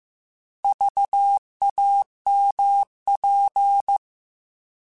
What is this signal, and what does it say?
Morse code, heard as a keyed tone, about 15 wpm, 800 Hz. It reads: VAMP
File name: Alphabetical / Alphabetical mp3s / Vs @13